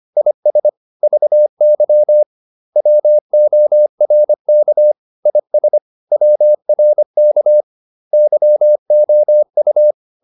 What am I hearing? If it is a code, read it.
Morse code, heard as a keyed tone, about 25 wpm, 600 Hz. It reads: IS VY WORK IS WRK YOU
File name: SS319 25